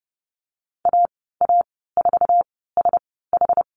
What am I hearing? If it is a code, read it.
Morse code, heard as a keyed tone, about 30 wpm, spaced as for 17 wpm, 700 Hz. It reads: AA4SH
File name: Lesson 14g - Number 2 - Call Signs 30wpm